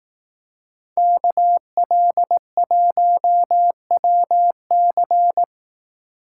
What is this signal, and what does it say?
Morse code, heard as a keyed tone, about 18 wpm, 700 Hz. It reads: KL1WC
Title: Speed Racing - USA 2x2 Call Signs 15wpm